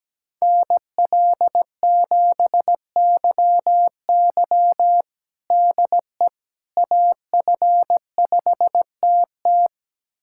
Morse code, heard as a keyed tone, about 17 wpm, 700 Hz. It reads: NL7YY DE AF5TT
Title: USA Callsign DE Callsign - 5 Characters - 17wpm